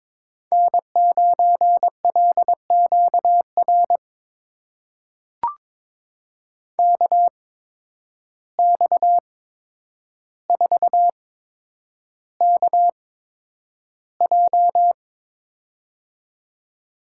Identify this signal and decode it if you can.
Morse code, heard as a keyed tone, about 22 wpm, spaced as for 9 wpm, 700 Hz. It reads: N9LQR KX4KJ
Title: USA Call Signs - 5 Characters - 8x Character Spacing - 22wpm